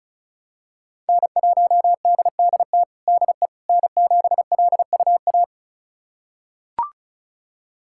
Morse code, heard as a keyed tone, about 34 wpm, 700 Hz. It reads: N1DDT DE N7LUA
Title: USA Callsign DE Callsign - 5 Characters - 35wpm